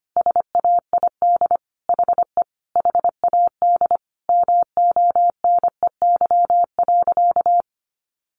Morse code, heard as a keyed tone, about 25 wpm, 700 Hz. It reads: SAID HE HAD MONEY.
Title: Sentences from Top 500 Words 25wpm